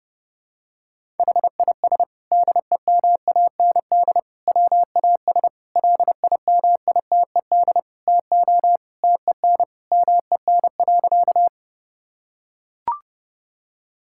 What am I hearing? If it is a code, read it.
Morse code, heard as a keyed tone, about 32 wpm, 700 Hz. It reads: HIS DEMAND WAS LIMITED TO TEN MEN.
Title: Sentences from Top 1000 Words - Review Part 6 - 30wpm